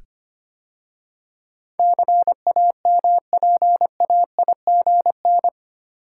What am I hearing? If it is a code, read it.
Morse code, heard as a keyed tone, about 25 wpm, 700 Hz. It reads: CAMPAIGN
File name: 8 Letter Words - First 3 Letters Given - 25wpm